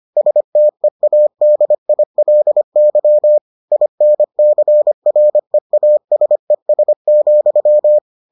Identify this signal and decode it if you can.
Morse code, heard as a keyed tone, about 25 wpm, 600 Hz. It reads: STEADILY INCREASES,